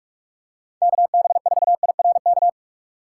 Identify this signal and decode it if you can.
Morse code, heard as a keyed tone, about 47 wpm, 700 Hz. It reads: KB4IRK
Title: USA Call Signs - 6 Characters - 2x Character Spacing - 45wpm